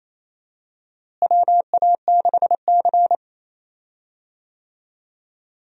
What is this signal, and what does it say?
Morse code, heard as a keyed tone, about 28 wpm, 700 Hz. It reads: WA6C
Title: USA Call Signs - 4 Characters - Repeated 3x - 28wpm